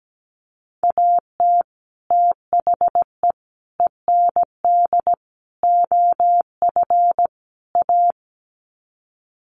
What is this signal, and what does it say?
Morse code, heard as a keyed tone, about 17 wpm, 700 Hz. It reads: AT THE END OF A
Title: Sets of 5 Words 17wpm